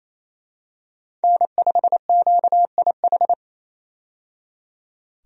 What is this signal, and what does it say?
Morse code, heard as a keyed tone, about 27 wpm, 700 Hz. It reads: N5QIH
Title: USA Call Signs - 5 Characters - 28wpm